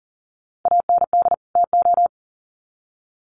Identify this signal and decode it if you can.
Morse code, heard as a keyed tone, about 40 wpm, 700 Hz. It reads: AND TO
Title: Sets of 2 Words 40wpm